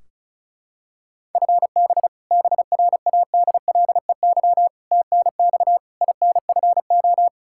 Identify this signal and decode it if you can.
Morse code, heard as a keyed tone, about 36 wpm, 700 Hz. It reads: FB BRADLEY TNX INFO
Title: Practice QSO - Exchange 3 - More Info - 35wpm